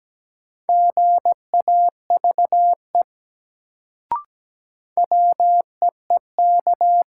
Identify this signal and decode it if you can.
Morse code, heard as a keyed tone, about 17 wpm, 700 Hz. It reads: GAVE WEEK
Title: Sentences from Top 500 Words - New Word Drill - 17wpm